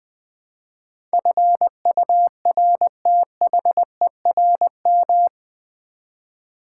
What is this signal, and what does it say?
Morse code, heard as a keyed tone, about 20 wpm, 700 Hz. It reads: FURTHERM
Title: Build A Word - 4 Letters At A Time - 11-Letter Words - 20wpm